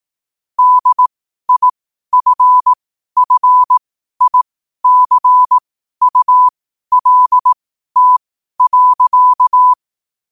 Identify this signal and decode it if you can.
Morse code, heard as a keyed tone, about 18 wpm, spaced as for 12 wpm, 1000 Hz. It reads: DIFFICULT.
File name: qotdfiles / 12wpm